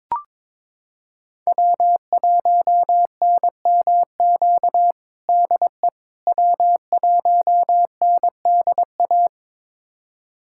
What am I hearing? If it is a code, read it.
Morse code, heard as a keyed tone, about 22 wpm, 700 Hz. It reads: W1NMQ DE W1NDA
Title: USA Callsign DE Callsign 22wpm